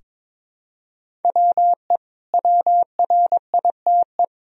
Morse code, heard as a keyed tone, about 22 wpm, 700 Hz. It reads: WE WRITE
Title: Sentences from Top 1000 Words - Review Part 3 - 22wpm